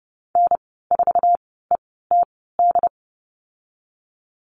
Morse code, heard as a keyed tone, about 29 wpm, spaced as for 17 wpm, 700 Hz. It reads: N4ETD
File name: Lesson 15g - Number 5 - Call Signs 30wpm